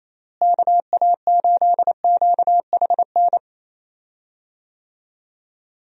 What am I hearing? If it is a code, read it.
Morse code, heard as a keyed tone, about 28 wpm, 700 Hz. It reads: KA8QHN
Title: USA Call Signs - 6 Characters - Repeated 3x - 28wpm